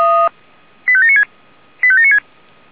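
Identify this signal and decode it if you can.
Five-tone selective call (ZVEI), heard as quick runs of five tones, tones of about 80 ms. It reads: 85796 85796
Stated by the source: ZVEI five-tone selective call received by SDR.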